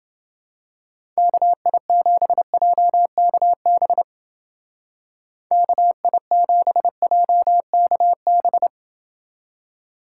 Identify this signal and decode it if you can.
Morse code, heard as a keyed tone, about 30 wpm, 700 Hz. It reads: KI7JKB KI7JKB